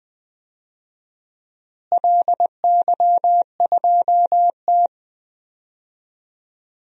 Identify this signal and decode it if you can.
Morse code, heard as a keyed tone, about 21 wpm, 700 Hz. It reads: LY2T
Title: DXCC-100 Call Signs - 4 Characters - Repeated 2x - 20wpm